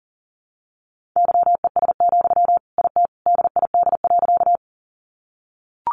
Morse code, heard as a keyed tone, about 40 wpm, 700 Hz. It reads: YES, IT DID.
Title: Sentences from Top 500 Words 40wpm